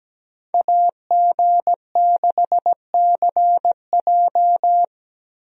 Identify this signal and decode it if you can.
Morse code, heard as a keyed tone, about 17 wpm, 700 Hz. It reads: AG6CJ
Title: USA Call Signs - 5 Characters - Repeated 3x - 17wpm